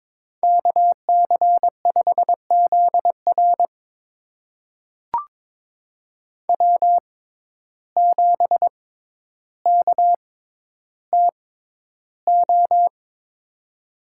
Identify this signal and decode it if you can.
Morse code, heard as a keyed tone, about 22 wpm, spaced as for 11 wpm, 700 Hz. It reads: KC5ZR W7KTO
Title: USA Call Signs - 6x Character Spacing - 22wpm